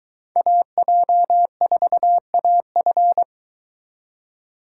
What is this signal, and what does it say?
Morse code, heard as a keyed tone, about 23 wpm, 700 Hz. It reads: AJ4AF